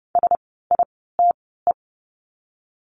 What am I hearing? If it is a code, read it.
Morse code, heard as a keyed tone, about 30 wpm, spaced as for 17 wpm, 700 Hz. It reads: SITE
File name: Lesson 17f - Letter u - Words 30wpm